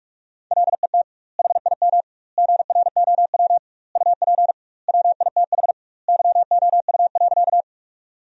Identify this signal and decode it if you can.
Morse code, heard as a keyed tone, about 42 wpm, 700 Hz. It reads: LET HIM GROW UP WITH YOU.
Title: 2x Word-Spacing - Sentences from Top 300 Words - 45wpm